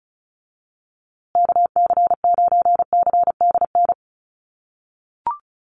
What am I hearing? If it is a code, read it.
Morse code, heard as a keyed tone, about 34 wpm, 700 Hz. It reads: KC9CDN
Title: USA Call Signs Repeated 2x 35wpm